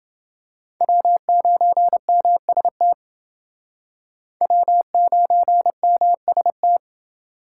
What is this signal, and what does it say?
Morse code, heard as a keyed tone, about 29 wpm, 700 Hz. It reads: W9MST W9MST